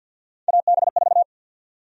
Morse code, heard as a keyed tone, about 47 wpm, 700 Hz. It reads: AB4
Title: USA Call Sign Prefixes - 50wpm